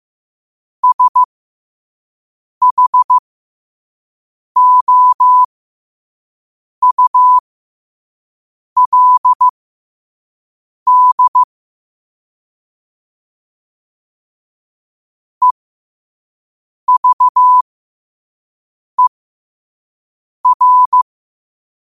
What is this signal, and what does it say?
Morse code, heard as a keyed tone, about 15 wpm, spaced as for 5 wpm, 1000 Hz. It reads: SHOULD EVER
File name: qotdfiles / 5wpm